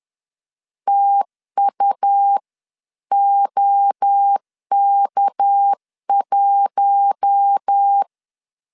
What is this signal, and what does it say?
Morse code, heard as a keyed tone, about 10 wpm, 800 Hz. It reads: TUOK1
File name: LEZIONI / 6 PKX0 / 6 PK 10wpm 2436